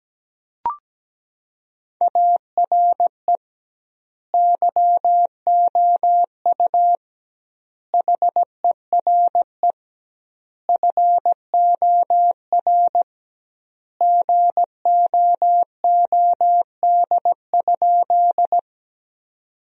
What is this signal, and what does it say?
Morse code, heard as a keyed tone, about 17 wpm, 700 Hz. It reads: ARE YOU HERE FOR GOOD?
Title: 2x Word-Spacing - Sentences from Top 200 Words - 17wpm